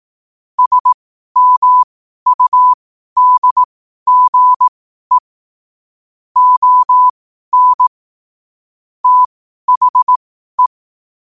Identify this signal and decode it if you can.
Morse code, heard as a keyed tone, about 18 wpm, spaced as for 12 wpm, 1000 Hz. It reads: SMUDGE ON THE
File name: qotdfiles / 12wpm